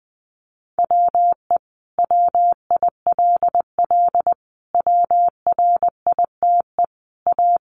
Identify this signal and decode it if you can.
Morse code, heard as a keyed tone, about 20 wpm, 700 Hz. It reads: WE WILL WRITE A
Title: Sentences from Top 1300 Words 20wpm